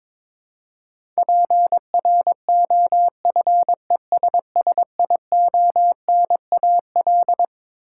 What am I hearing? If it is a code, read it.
Morse code, heard as a keyed tone, about 22 wpm, 700 Hz. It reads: PROFESSIONAL
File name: Twelve Letter Words - 22wpm